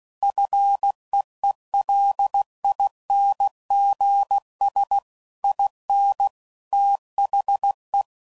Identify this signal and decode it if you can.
Morse code, heard as a keyed tone, about 16 wpm, 800 Hz. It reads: FEELINGS IN THE